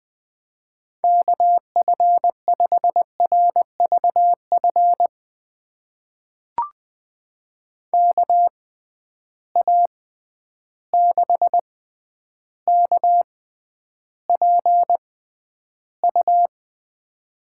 Morse code, heard as a keyed tone, about 20 wpm, spaced as for 10 wpm, 700 Hz. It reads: KF5RVF KA6KPU
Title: USA Call Signs - 6 Characters - 6x Character Spacing - 20wpm